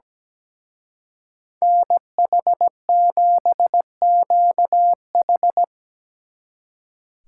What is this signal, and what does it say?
Morse code, heard as a keyed tone, about 18 wpm, 700 Hz. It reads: NH7QH
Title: Speed Racing - USA 2x2 Call Signs 17wpm